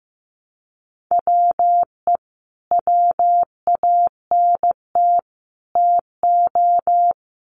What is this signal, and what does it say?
Morse code, heard as a keyed tone, about 15 wpm, 700 Hz. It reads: WE WANT TO
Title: Sets of 3 Words 15wpm